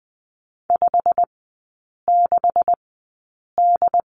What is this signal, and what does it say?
Morse code, heard as a keyed tone, about 20 wpm, spaced as for 8 wpm, 700 Hz. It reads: 56D
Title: 152 Lesson 25e - Letter p - Three Characters 20wpm